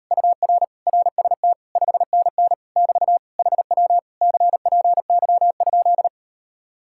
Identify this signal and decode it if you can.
Morse code, heard as a keyed tone, about 38 wpm, 700 Hz. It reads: UR RST 5NN <BT> HW CPY?